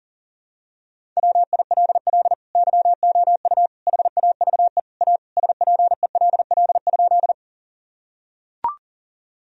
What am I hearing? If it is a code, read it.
Morse code, heard as a keyed tone, about 37 wpm, 700 Hz. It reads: WILL YOU HAVE A SPELL?
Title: Sentences from Top 1000 Words - Review Part 1 - 40wpm